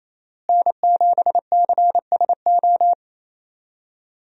Morse code, heard as a keyed tone, about 28 wpm, 700 Hz. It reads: N7CSO
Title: USA Call Signs - 5 Characters - Repeated 3x - 28wpm